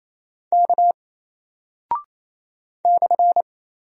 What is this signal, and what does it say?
Morse code, heard as a keyed tone, about 27 wpm, 700 Hz. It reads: K /
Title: Single Character (Letter:Number:Punctuation:Prosign) 28wpm